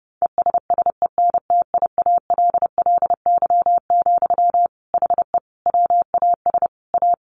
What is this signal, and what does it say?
Morse code, heard as a keyed tone, about 30 wpm, 700 Hz. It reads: ESSENTIALLY, HE WAS A
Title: Sentences from Top 1300 Words 30wpm